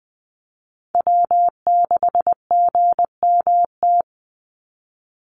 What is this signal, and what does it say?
Morse code, heard as a keyed tone, about 20 wpm, 700 Hz. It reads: W6GMT
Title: Call Signs - 20wpm